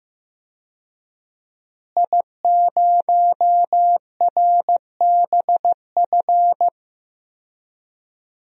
Morse code, heard as a keyed tone, about 15 wpm, 700 Hz. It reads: I0RBF
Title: DXCC-100 Call Signs - 5 Characters - Repeated 2x - 15wpm